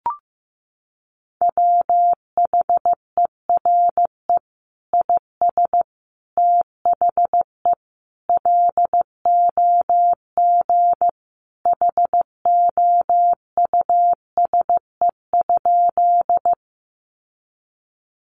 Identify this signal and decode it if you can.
Morse code, heard as a keyed tone, about 15 wpm, 700 Hz. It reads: WHERE IS THE LOG HOUSE?
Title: Sentences from Top 1400 Words 15wpm